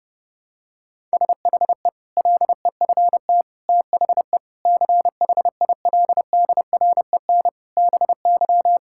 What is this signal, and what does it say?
Morse code, heard as a keyed tone, about 32 wpm, 700 Hz. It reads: SHE LEFT THE CHILDREN BY
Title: Sentences from Top 1000 Words - Review Part 5 - 30wpm